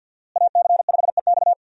Morse code, heard as a keyed tone, about 52 wpm, 700 Hz. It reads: AC5EX
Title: USA Call Signs - 5 Characters - Repeated 2x - 50wpm